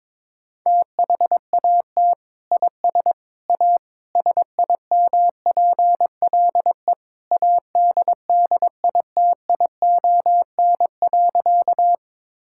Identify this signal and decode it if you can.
Morse code, heard as a keyed tone, about 22 wpm, 700 Hz. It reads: THAT IS A SIMPLE ADDITION.
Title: Sentences from Top 1000 Words - Review Part 6 - 22wpm